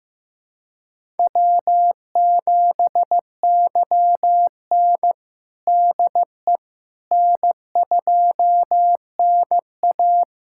Morse code, heard as a keyed tone, about 15 wpm, 700 Hz. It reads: W7YN DE N2NA
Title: USA Callsign DE Callsign - 4 Characters - 15wpm